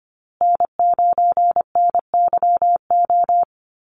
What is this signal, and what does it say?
Morse code, heard as a keyed tone, about 25 wpm, 700 Hz. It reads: N9NYO
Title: USA Call Signs Repeated 2x 25wpm